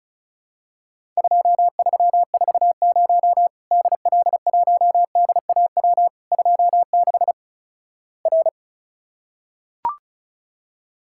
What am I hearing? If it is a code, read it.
Morse code, heard as a keyed tone, about 36 wpm, 700 Hz. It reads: 2340 DL1DAW 26 R
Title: Worked All Europe DX Contest - QTC Exchange - 35wpm